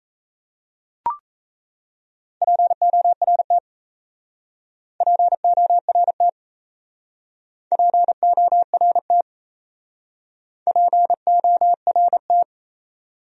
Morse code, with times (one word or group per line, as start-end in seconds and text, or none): 2.4-3.6 PORT
5.0-6.3 PORT
7.7-9.2 PORT
10.7-12.4 PORT